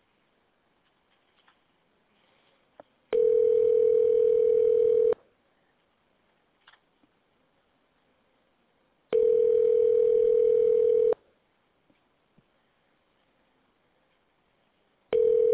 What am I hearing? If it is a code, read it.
Ringback tone (North America), 440 + 480 Hz.